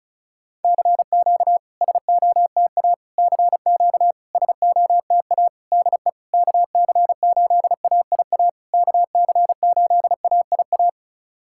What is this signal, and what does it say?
Morse code, heard as a keyed tone, about 36 wpm, 700 Hz. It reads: CQ SOTA CQ SOTA DE KC8AIA KC8AIA
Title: SOTA QSO Practice 35wpm